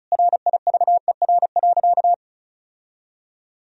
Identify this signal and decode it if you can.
Morse code, heard as a keyed tone, about 34 wpm, 700 Hz. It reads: RIVER.
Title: Sentences from Top 1000 Words - Review Part 1 - 35wpm